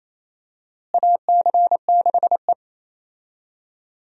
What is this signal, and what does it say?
Morse code, heard as a keyed tone, about 28 wpm, 700 Hz. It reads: AC6E
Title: USA Call Signs - 4 Characters - Repeated 2x - 28wpm